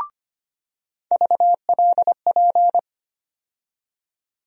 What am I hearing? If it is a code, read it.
Morse code, heard as a keyed tone, about 25 wpm, 700 Hz. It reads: VLP
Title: USA Call Sign Postfixes - 3 Letters 25wpm